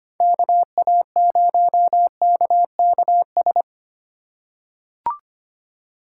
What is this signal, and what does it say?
Morse code, heard as a keyed tone, about 25 wpm, 700 Hz. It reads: KA0KKS
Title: USA Call Signs - 6 Characters - 2x Character Spacing - 25wpm